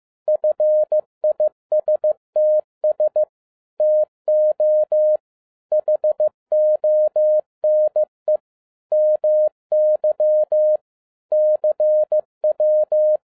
Morse code, heard as a keyed tone, about 15 wpm, 610 Hz. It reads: FISTS TO HONE MY CW
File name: 2011-07-mp3 / n02 JF2PDA 15wpm